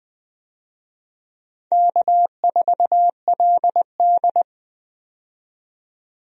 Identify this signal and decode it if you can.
Morse code, heard as a keyed tone, about 20 wpm, 700 Hz. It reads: K4LD
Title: USA Call Signs - 4 Characters - 20wpm